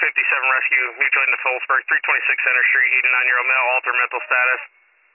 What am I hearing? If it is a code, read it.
Emergency Alert System (SAME) tones, heard as harsh buzzing bursts, with speech. It is heard from 0.6 to 4.6 s.